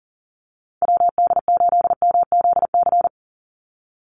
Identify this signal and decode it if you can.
Morse code, heard as a keyed tone, about 40 wpm, 700 Hz. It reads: WD8MZC
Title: USA Call Signs 40wpm